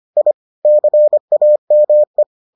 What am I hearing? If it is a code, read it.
Morse code, heard as a keyed tone, about 25 wpm, 600 Hz. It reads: I CAME